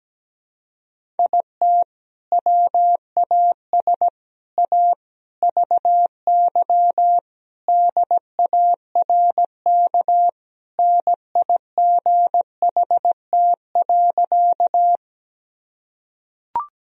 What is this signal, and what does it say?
Morse code, heard as a keyed tone, about 17 wpm, 700 Hz. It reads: IT WAS A VY DARK NIGHT.
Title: Sentences from Top 500 Words and Abbreviations 17wpm